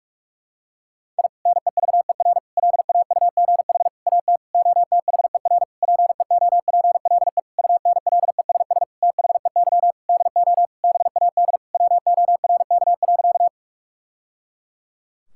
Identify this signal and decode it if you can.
Morse code, heard as a keyed tone, about 46 wpm, 700 Hz. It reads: I NEVER LAUGH AT OTHER PEOPLE UNLESS THEY DO BAD WORK.
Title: Sentences from Top 1000 Words - Review Part 6 - 45wpm